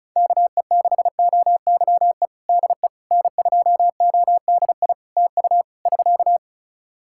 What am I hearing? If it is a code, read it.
Morse code, heard as a keyed tone, about 36 wpm, 700 Hz. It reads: KE6OYE DE N2ODI TU <SK>
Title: K1USN SST QSO Practice 35wpm